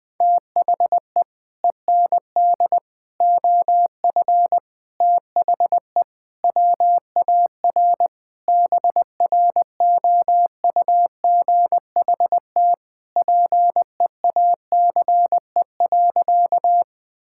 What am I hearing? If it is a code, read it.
Morse code, heard as a keyed tone, about 20 wpm, 700 Hz. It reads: THE END OF THE WAR BROUGHT PEACE.
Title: Sentences with 5-Word Phrases - Spoken Beginning - 20wpm